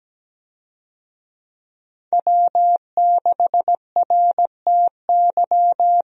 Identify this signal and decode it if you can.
Morse code, heard as a keyed tone, about 17 wpm, 700 Hz. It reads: W6RTY
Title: USA Call Signs - 5 Characters - 17wpm